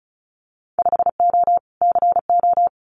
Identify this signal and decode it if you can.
Morse code, heard as a keyed tone, about 36 wpm, 700 Hz. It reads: 5O CO
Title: Winter Field Day Exchanges Encore 35wpm